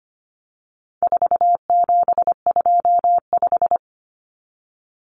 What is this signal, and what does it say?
Morse code, heard as a keyed tone, about 25 wpm, 700 Hz. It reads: 4725
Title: Numbers - 4-Digit 25wpm